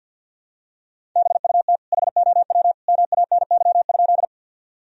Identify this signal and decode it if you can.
Morse code, heard as a keyed tone, about 53 wpm, 700 Hz. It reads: BUT HOW MANY?
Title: Sentences from Top 1000 Words - Review Part 1 - 50wpm